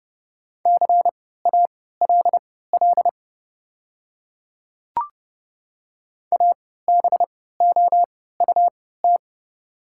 Morse code, heard as a keyed tone, about 30 wpm, spaced as for 17 wpm, 690 Hz. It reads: CALL ABOUT
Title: 3x Character Spacing - Top 500 Words - 30wpm